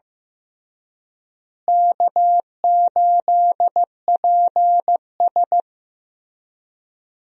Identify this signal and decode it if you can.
Morse code, heard as a keyed tone, about 16 wpm, 700 Hz. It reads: K8PS
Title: USA Call Signs - 4 Characters - Repeated 3x - 15wpm